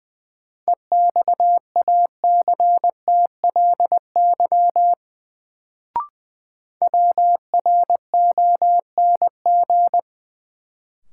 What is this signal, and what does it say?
Morse code, heard as a keyed tone, about 21 wpm, 700 Hz. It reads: EXACTLY WRONG
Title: Sentences from Top 900 Words - New Word Drill - 20wpm